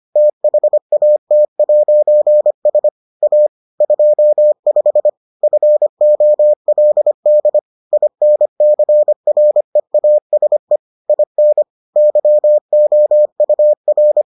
Morse code, heard as a keyed tone, about 25 wpm, 600 Hz. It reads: THAT'S A 25 FOLD INCREASE IN YOUR